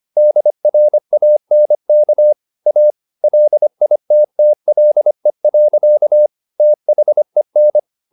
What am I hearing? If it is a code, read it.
Morse code, heard as a keyed tone, about 25 wpm, 600 Hz. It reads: DRANK A LITTLE. THEN
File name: the old man and the sea part 1 25wpm